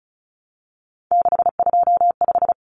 Morse code, heard as a keyed tone, about 34 wpm, 700 Hz. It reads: 625
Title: Common Cut Numbers - 3-Digit 35wpm